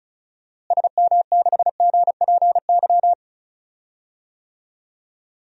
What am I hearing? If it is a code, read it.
Morse code, heard as a keyed tone, about 36 wpm, 700 Hz. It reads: SM6GPY
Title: DXCC-100 Call Signs - 6 Characters - Repeated 3x - 35wpm